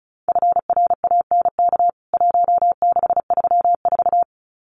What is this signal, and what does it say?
Morse code, heard as a keyed tone, about 34 wpm, 700 Hz. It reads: FRANK 1634
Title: CWT Exchange 35wpm